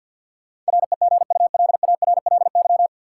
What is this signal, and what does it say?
Morse code, heard as a keyed tone, about 49 wpm, 700 Hz. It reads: REGULARLY
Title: Sentences from Top 1400 Words - New Word Drill - 50wpm